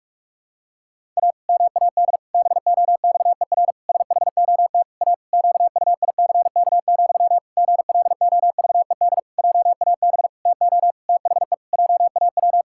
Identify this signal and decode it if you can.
Morse code, heard as a keyed tone, about 41 wpm, 700 Hz. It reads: A MAD BOXER SHOT A QUICK, GLOVED JAB TO THE JAW
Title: Pangram Challenge v2 - 45wpm